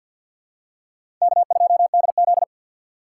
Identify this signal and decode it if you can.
Morse code, heard as a keyed tone, about 51 wpm, 700 Hz. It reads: K2DZ
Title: USA Call Signs - 4 Characters - 6x Character Spacing - 50wpm